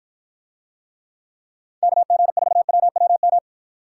Morse code, heard as a keyed tone, about 54 wpm, 700 Hz. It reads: KG4WWM